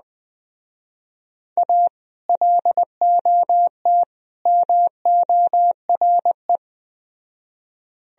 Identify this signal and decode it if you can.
Morse code, heard as a keyed tone, about 20 wpm, 700 Hz. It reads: A LOT MORE
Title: Speed-Racing - Sets of 3 Words 17wpm